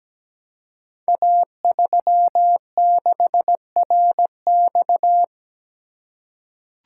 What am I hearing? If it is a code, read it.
Morse code, heard as a keyed tone, about 17 wpm, 700 Hz. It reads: A36RX
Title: Code Groups - 5 Letter:Numbers - 17wpm